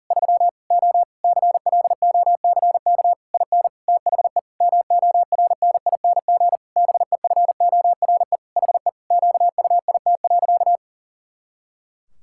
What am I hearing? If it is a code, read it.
Morse code, heard as a keyed tone, about 40 wpm, 700 Hz. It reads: 3 O CLOCK IN THE MORNING BEFORE HE QUIT.
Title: Pangram Challenge v2 - 40wpm